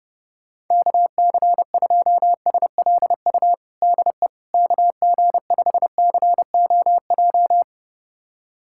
Morse code, heard as a keyed tone, about 30 wpm, 700 Hz. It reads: KC2SLU DE KG5COJ
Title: USA Callsign DE Callsign - 6 Characters - 30wpm